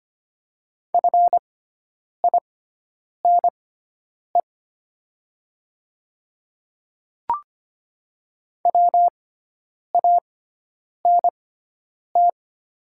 Morse code, heard as a keyed tone, about 25 wpm, spaced as for 9 wpm, 700 Hz. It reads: FINE WANT
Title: 6x Character Spacing - Top 500 Words - 25wpm